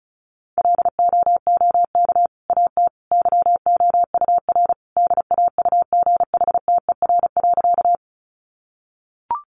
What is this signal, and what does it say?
Morse code, heard as a keyed tone, about 34 wpm, 700 Hz. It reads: LOOK AT YOUR DAUGHTER.
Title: Sentences from Top 1500 Words 35wpm